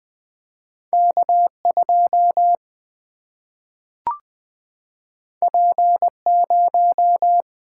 Morse code, heard as a keyed tone, about 20 wpm, 700 Hz. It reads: K2 P0
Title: Single-Letter Single-Number 20wpm